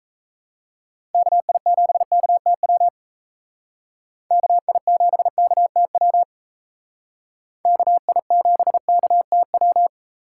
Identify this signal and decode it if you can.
Morse code, heard as a keyed tone, about 37 wpm, 700 Hz. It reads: KI7KTW KI7KTW KI7KTW